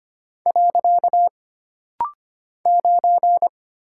Morse code, heard as a keyed tone, about 25 wpm, 700 Hz. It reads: . 9
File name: Single Character (Letter:Number:Punctuation:Prosign) - Rapid-Fire - 25wpm